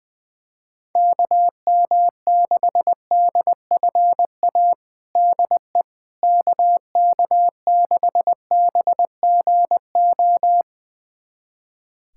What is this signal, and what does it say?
Morse code, heard as a keyed tone, about 20 wpm, 700 Hz. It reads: KM6DFA DE KK6BGO
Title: USA Callsign DE Callsign - 6 Characters - 20wpm